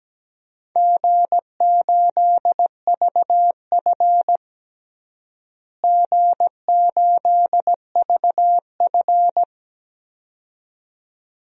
Morse code, heard as a keyed tone, about 17 wpm, 700 Hz. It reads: G8VF G8VF
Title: DXCC-100 Call Signs - 4 Characters - Repeated 2x - 17wpm